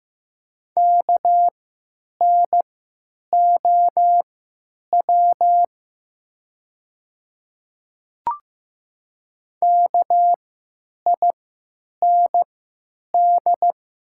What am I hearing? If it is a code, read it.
Morse code, heard as a keyed tone, about 15 wpm, spaced as for 9 wpm, 700 Hz. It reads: KNOW KIND
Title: 3x Character Spacing - Top 200 Words - 15wpm